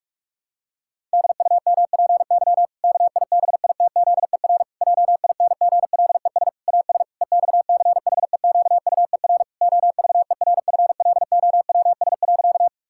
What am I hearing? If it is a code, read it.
Morse code, heard as a keyed tone, about 41 wpm, 700 Hz. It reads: DUMPY KIBITZER JINGLES AS EXCHEQUER OVERFLOWS.
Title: Pangram Challenge v2 - 45wpm